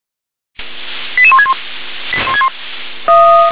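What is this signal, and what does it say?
Five-tone selective call (ZVEI), heard as quick runs of five tones, tones of about 70 ms. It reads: 88161
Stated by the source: ZVEI five-tone selective call received on SDR.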